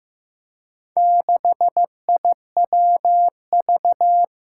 Morse code, heard as a keyed tone, about 15 wpm, 700 Hz. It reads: 6IWV
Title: Four Challenging Letter-Number Combinations 15wpm